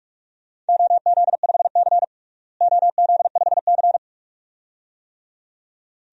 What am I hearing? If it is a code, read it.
Morse code, heard as a keyed tone, about 46 wpm, 700 Hz. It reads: OZ5C OZ5C
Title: DXCC-100 Call Signs - 4 Characters - Repeated 2x - 45wpm